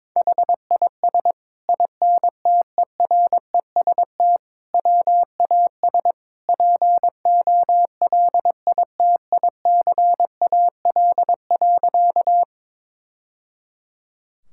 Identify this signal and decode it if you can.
Morse code, heard as a keyed tone, about 22 wpm, 700 Hz. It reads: HIS INTEREST WAS POLITICAL.
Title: Sentences from Top 1000 Words - Review Part 5 - 22wpm